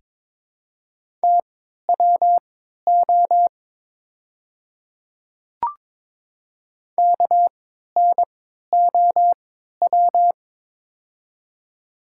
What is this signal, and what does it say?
Morse code, heard as a keyed tone, about 22 wpm, spaced as for 12 wpm, 700 Hz. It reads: TWO KNOW
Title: 3x Character Spacing - Top 300 Words - 22wpm